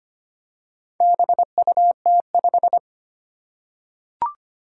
Morse code, heard as a keyed tone, about 25 wpm, 700 Hz. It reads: BUT5
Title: Four Challenging Letter-Number Combinations 25wpm